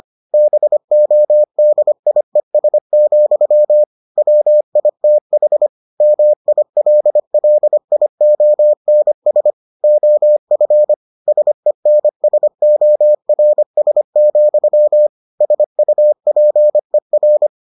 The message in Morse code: BODIES, WITH MILLIONS OF SENSORS, SUPER

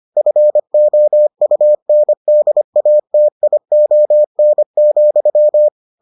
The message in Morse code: FOUNDATION,

